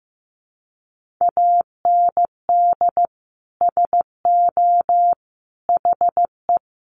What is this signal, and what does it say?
Morse code, heard as a keyed tone, about 15 wpm, 700 Hz. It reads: AND SO HE
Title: Sentences from Top 1500 Words 15wpm